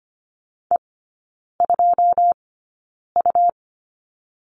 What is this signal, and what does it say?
Morse code, heard as a keyed tone, about 25 wpm, spaced as for 9 wpm, 700 Hz. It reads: E2U
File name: Lesson 17e - Letter u - Three Characters 25wpm